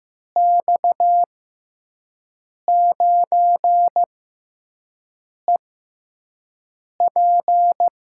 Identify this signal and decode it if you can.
Morse code, heard as a keyed tone, about 15 wpm, spaced as for 5 wpm, 700 Hz. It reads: X9EP
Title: USA Call Signs - 5 Characters - 6x Character Spacing - 15wpm